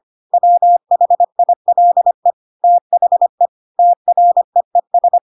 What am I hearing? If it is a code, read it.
Morse code, heard as a keyed tone, about 25 wpm, 700 Hz. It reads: WHILE THE TREES
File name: Pinocchio / Pinocchio ch6 25 25 ik1hge